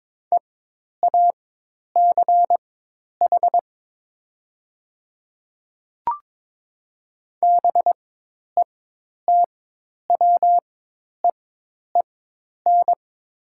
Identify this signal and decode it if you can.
Morse code, heard as a keyed tone, about 22 wpm, spaced as for 10 wpm, 700 Hz. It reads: EACH BETWEEN
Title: 4x Character Spacing - Top 300 Words - 22wpm